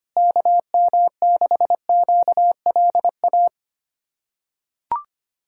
Morse code, heard as a keyed tone, about 25 wpm, 700 Hz. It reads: KM6QLA
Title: USA Callsign DE Callsign 25wpm